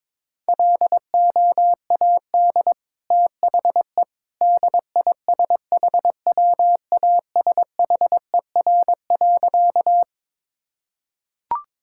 Morse code, heard as a keyed tone, about 22 wpm, 700 Hz. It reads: LOAD THE DISHWASHER.
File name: practice / 22wpm / Easy Sentences with Hints - 22wpm